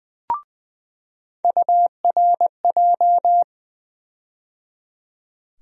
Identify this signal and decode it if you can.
Morse code, heard as a keyed tone, about 20 wpm, 700 Hz. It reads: URJ
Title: USA Call Sign Postfixes - 3 Letters 20wpm